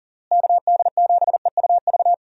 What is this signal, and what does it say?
Morse code, heard as a keyed tone, about 40 wpm, 700 Hz. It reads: KD7EUV
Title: USA Call Signs - 6 Characters - Repeated 3x - 40wpm